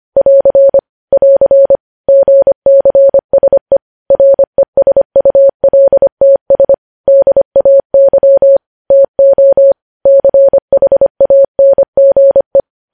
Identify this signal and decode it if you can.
Morse code, heard as a keyed tone, about 25 wpm, 550 Hz. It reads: <AR> <AR> GCSE RESULTS DAY TO CHANGE
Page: news headlines in morse code at 25 wpm